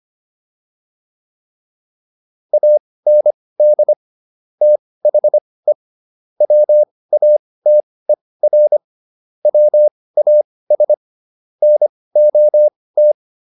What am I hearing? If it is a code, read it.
Morse code, heard as a keyed tone, about 25 wpm, spaced as for 18 wpm, 600 Hz. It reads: AND THE WATER WAS NOT